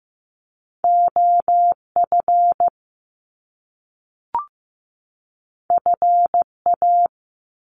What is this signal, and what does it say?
Morse code, heard as a keyed tone, about 15 wpm, 700 Hz. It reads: OF FA
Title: Top Two Letter Combinations in Words 15wpm